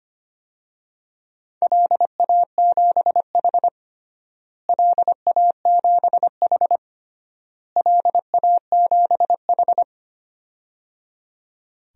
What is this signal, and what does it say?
Morse code, heard as a keyed tone, about 25 wpm, 700 Hz. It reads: LA7H LA7H LA7H
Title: DXCC-100 Call Signs - 4 Characters - Repeated 3x - 25wpm